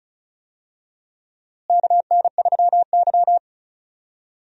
Morse code, heard as a keyed tone, about 34 wpm, 700 Hz. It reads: KN3Y
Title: USA Call Signs - 4 Characters - Repeated 3x - 35wpm